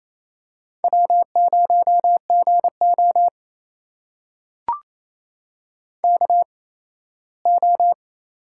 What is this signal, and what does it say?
Morse code, heard as a keyed tone, about 28 wpm, spaced as for 17 wpm, 700 Hz. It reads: W0GO KO
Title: USA Call Signs - 4 Characters - 8x Character Spacing - 28wpm